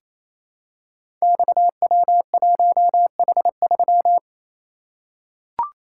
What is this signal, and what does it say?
Morse code, heard as a keyed tone, about 28 wpm, 700 Hz. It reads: XW1H3
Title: Code Groups - 5 Letter:Numbers - 28wpm